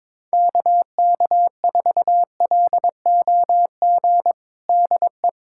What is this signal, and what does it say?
Morse code, heard as a keyed tone, about 22 wpm, 700 Hz. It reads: KK4LOG DE
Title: USA Callsign DE Callsign - 6 Characters - 22wpm